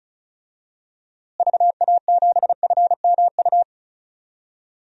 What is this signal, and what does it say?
Morse code, heard as a keyed tone, about 34 wpm, 700 Hz. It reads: VA7FMU
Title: DXCC-100 Call Signs - 6 Characters - Repeated 2x - 35wpm